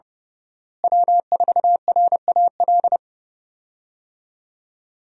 Morse code, heard as a keyed tone, about 30 wpm, 700 Hz. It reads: W4RAL
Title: USA Call Signs - 5 Characters - Repeated 2x - 30wpm